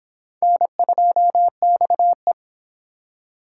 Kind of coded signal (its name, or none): Morse code